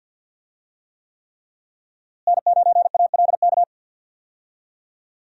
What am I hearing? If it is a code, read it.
Morse code, heard as a keyed tone, about 46 wpm, 700 Hz. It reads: N9ALK
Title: USA Call Signs - 5 Characters - Repeated 3x - 50wpm